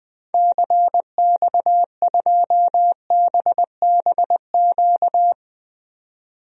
Morse code, heard as a keyed tone, about 21 wpm, 700 Hz. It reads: CX2BBQ
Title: DXCC-100 Call Signs - 6 Characters - 20wpm